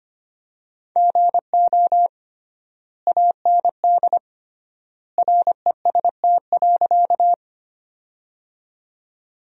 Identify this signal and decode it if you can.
Morse code, heard as a keyed tone, about 25 wpm, 700 Hz. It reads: GO AND REST.
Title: 3x Word-Spacing - Sentences from Top 500 Words - 25wpm